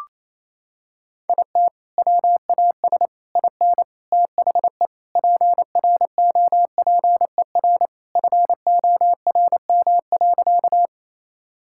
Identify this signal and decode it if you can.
Morse code, heard as a keyed tone, about 28 wpm, 700 Hz. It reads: IT WAS IN THE PROPER FORM.
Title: Sentences from Top 1000 Words - Review Part 5 - 28wpm